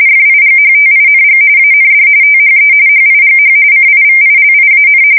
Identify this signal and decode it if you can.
RTTY, about 45.45 baud. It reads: FIGHTING AND THE HIGH CIVILIAN
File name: fsk170 / usnavy4